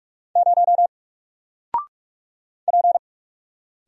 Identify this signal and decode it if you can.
Morse code, heard as a keyed tone, about 46 wpm, 700 Hz. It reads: 0 P
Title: Single Letter-Number 45wpm